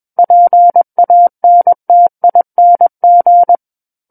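Morse code, heard as a keyed tone, about 21 wpm, 700 Hz. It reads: PANTING
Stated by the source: Morse code practice reading of a book chapter at 21 words per minute.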